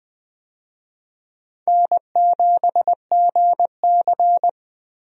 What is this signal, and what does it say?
Morse code, heard as a keyed tone, about 20 wpm, 700 Hz. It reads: N7GC
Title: USA Call Signs - 4 Characters - Repeated 2x - 20wpm